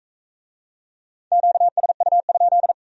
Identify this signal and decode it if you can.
Morse code, heard as a keyed tone, about 38 wpm, 700 Hz. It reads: QSU?